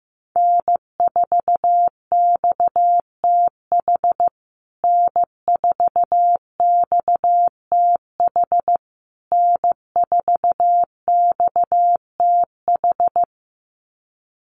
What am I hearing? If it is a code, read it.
Morse code, heard as a keyed tone, about 15 wpm, 700 Hz. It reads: N4XTH N4XTH N4XTH
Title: USA Call Signs Repeated 3x 15wpm